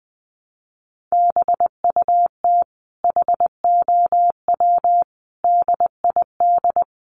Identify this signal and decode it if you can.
Morse code, heard as a keyed tone, about 20 wpm, 700 Hz. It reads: BUT HOW DID
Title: Sentences from Top 1400 Words 20wpm